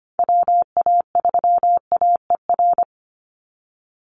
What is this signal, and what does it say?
Morse code, heard as a keyed tone, about 25 wpm, 700 Hz. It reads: WA3AER
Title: Call Signs Repeated 3x - 25wpm